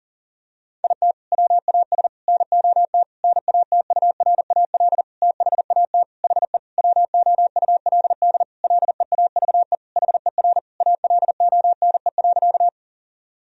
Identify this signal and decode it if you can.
Morse code, heard as a keyed tone, about 40 wpm, 700 Hz. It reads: IT WAS NOT NATURAL THAT HE WOULD LEAVE HER ALONE.
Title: Sentences from Top 1000 Words - Review Part 4 - 40wpm